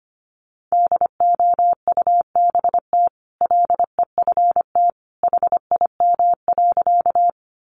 Morse code, heard as a keyed tone, about 25 wpm, 700 Hz. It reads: DOUBT LEFT HIM.
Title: Sentences from Top 1500 Words 25wpm